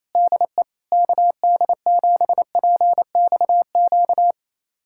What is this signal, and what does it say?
Morse code, heard as a keyed tone, about 28 wpm, 700 Hz. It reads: DE KD7PXQ
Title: USA Callsign DE Callsign - 6 Characters - 28wpm